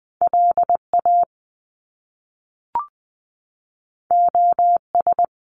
Morse code, heard as a keyed tone, about 20 wpm, 700 Hz. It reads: LA OS
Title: Top Two Letter Combinations in Words 20wpm